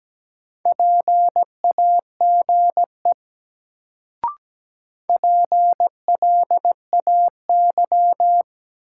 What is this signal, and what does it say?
Morse code, heard as a keyed tone, about 17 wpm, 700 Hz. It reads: PAGE PLAY
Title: Sets of 5 Words Encore - New Word Drill - 17wpm